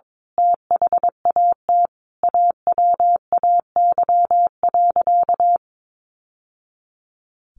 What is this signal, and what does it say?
Morse code, heard as a keyed tone, about 22 wpm, 700 Hz. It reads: THAT AWAY.
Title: Sentences from Top 600 Words 22wpm